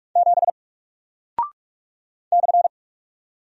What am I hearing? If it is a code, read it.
Morse code, heard as a keyed tone, about 45 wpm, 700 Hz. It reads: 7 /
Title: Single Character (Letter:Number:Punctuation:Prosign) Alt 45wpm